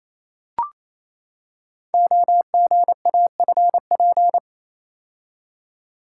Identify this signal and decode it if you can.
Morse code, heard as a keyed tone, about 28 wpm, 700 Hz. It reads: OGAFP
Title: Code Groups - 5 Letters - 28wpm